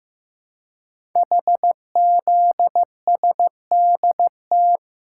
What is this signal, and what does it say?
Morse code, heard as a keyed tone, about 15 wpm, 700 Hz. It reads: HZSDT
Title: Code Groups - 5 Letters - 15wpm